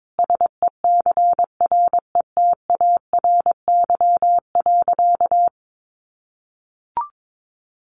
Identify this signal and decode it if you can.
Morse code, heard as a keyed tone, about 22 wpm, 700 Hz. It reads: SECRETARY.
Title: Sentences from Top 1300 Words 22wpm